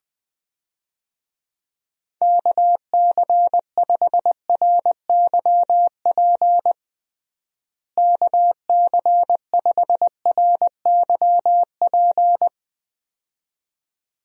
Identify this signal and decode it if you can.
Morse code, heard as a keyed tone, about 21 wpm, 700 Hz. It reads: KC5RYP KC5RYP
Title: USA Call Signs - 6 Characters - Repeated 2x - 20wpm